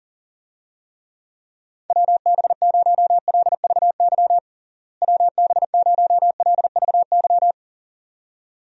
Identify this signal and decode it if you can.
Morse code, heard as a keyed tone, about 40 wpm, 700 Hz. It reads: WB0LVY WB0LVY
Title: USA Call Signs - 6 Characters - Repeated 2x - 40wpm